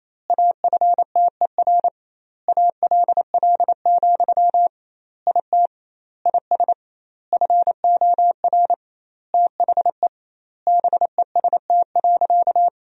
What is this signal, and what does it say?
Morse code, heard as a keyed tone, about 28 wpm, 700 Hz. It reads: AFTER ALL, IT IS FOR THE BEST.
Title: 2x Word-Spacing - Sentences from Top 500 Words - 28wpm